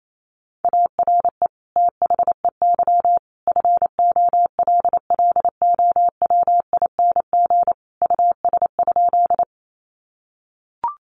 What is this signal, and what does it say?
Morse code, heard as a keyed tone, about 28 wpm, 700 Hz. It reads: ARE THEY FOLLOWING US?
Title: Sentences from Top 700 Words 28wpm